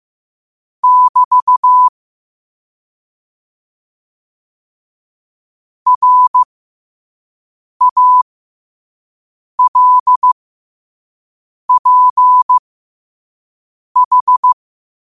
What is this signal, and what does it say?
Morse code, heard as a keyed tone, about 15 wpm, spaced as for 5 wpm, 1000 Hz. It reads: <BT> RALPH
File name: qotdfiles / 5wpm